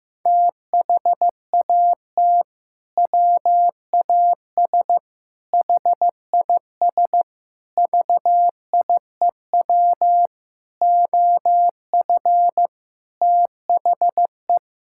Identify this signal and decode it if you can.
Morse code, heard as a keyed tone, about 15 wpm, 700 Hz. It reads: THAT WAS HIS VIEW OF THE
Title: Sentences from Top 1000 Words - Review Part 6 - 15wpm